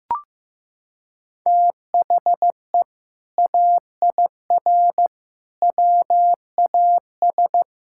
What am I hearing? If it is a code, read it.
Morse code, heard as a keyed tone, about 16 wpm, 700 Hz. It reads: THE AIR WAS
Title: Sentences from Top 1600 Words 15wpm